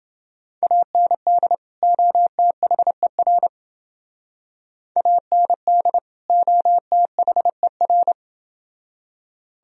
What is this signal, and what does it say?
Morse code, heard as a keyed tone, about 29 wpm, 700 Hz. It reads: AND OTHER AND OTHER